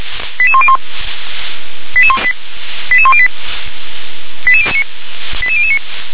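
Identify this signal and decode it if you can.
Five-tone selective call (ZVEI), heard as quick runs of five tones, tones of about 70 ms. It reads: 88101 88108 88009
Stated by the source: ZVEI five-tone selective call received on an SDR.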